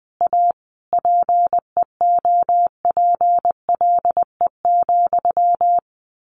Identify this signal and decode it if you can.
Morse code, heard as a keyed tone, about 20 wpm, 700 Hz. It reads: A PEOPLE,
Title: A Sound Of Thunder 20wpm